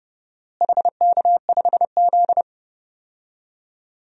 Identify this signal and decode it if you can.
Morse code, heard as a keyed tone, about 31 wpm, 700 Hz. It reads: HK5Z
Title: DXCC-100 Call Signs - 4 Characters - Repeated 2x - 30wpm